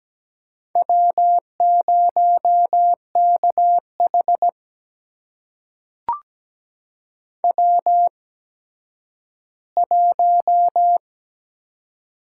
Morse code, heard as a keyed tone, about 17 wpm, spaced as for 10 wpm, 700 Hz. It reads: W0KH W1
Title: USA Call Signs - 4 Characters - 8x Character Spacing - 17wpm